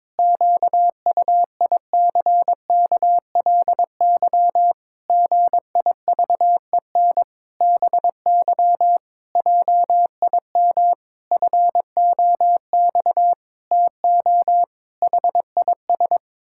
Morse code, heard as a keyed tone, about 22 wpm, 700 Hz. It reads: QUICKLY GIVEN BY JIM FOX TO HIS
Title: Pangram Challenge v2 - 22wpm